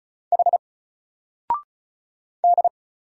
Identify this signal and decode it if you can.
Morse code, heard as a keyed tone, about 36 wpm, 700 Hz. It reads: H D
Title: Single Letter-Number Alt 35wpm